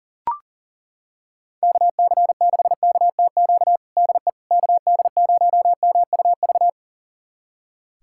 Morse code, heard as a keyed tone, about 40 wpm, 700 Hz. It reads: KC6KTQ DE KD0MUV
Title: USA Callsign DE Callsign 40wpm